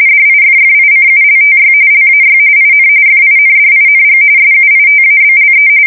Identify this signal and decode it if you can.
RTTY, about 45.45 baud. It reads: DIFFICULTY LAY IN FINDING SUITABLE